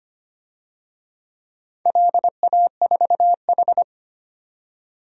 Morse code, heard as a keyed tone, about 25 wpm, 700 Hz. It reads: LA4H
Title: DXCC-100 Call Signs - 4 Characters - 25wpm